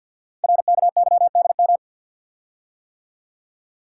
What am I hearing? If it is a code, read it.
Morse code, heard as a keyed tone, about 51 wpm, 700 Hz. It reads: RKYDM
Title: Code Groups - 5 Letters - 50wpm